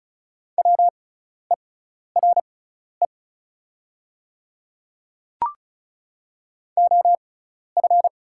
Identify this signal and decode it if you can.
Morse code, heard as a keyed tone, about 35 wpm, spaced as for 12 wpm, 690 Hz. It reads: WERE OF
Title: 6x Character Spacing - Top 100 Words - 35wpm